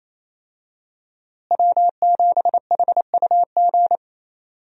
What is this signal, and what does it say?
Morse code, heard as a keyed tone, about 28 wpm, 700 Hz. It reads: W7HUG
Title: USA Call Signs - 5 Characters - Repeated 2x - 28wpm